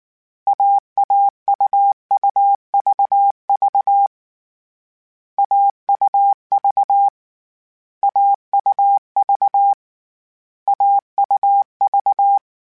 Morse code, heard as a keyed tone, about 19 wpm, 800 Hz. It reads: AAUUVV AUV AUV AUV